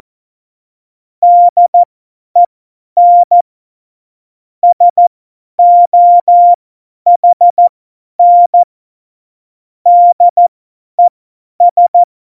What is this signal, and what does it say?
Morse code, heard as a keyed tone, about 14 wpm, spaced as for 10 wpm, 700 Hz. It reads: DEN SOHN DES
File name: Morsekurs / Witze / Witz 21 Juli 2020, 50 F70